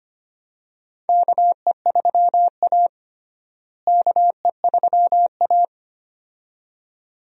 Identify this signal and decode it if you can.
Morse code, heard as a keyed tone, about 25 wpm, 700 Hz. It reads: KE3A KE3A
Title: USA Call Signs - 4 Characters - Repeated 2x - 25wpm